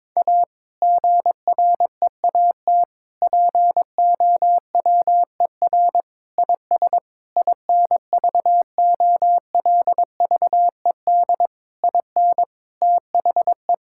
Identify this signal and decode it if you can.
Morse code, heard as a keyed tone, about 22 wpm, 700 Hz. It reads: A GREAT POWER IS INVOLVED IN THE
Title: Sentences from Top 1000 Words - Review Part 5 - 22wpm